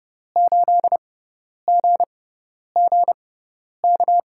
Morse code, heard as a keyed tone, about 31 wpm, spaced as for 10 wpm, 700 Hz. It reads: 8GGK
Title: USA Call Signs - 6 Characters - 6x Character Spacing - 30wpm